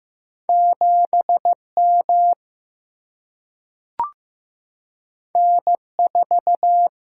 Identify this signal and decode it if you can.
Morse code, heard as a keyed tone, about 15 wpm, 700 Hz. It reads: 7 M N 4
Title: Single-Letter Single-Number 15wpm